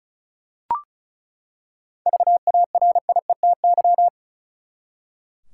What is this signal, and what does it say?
Morse code, heard as a keyed tone, about 36 wpm, 700 Hz. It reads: VARIETY
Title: Seven Letter Words - 35wpm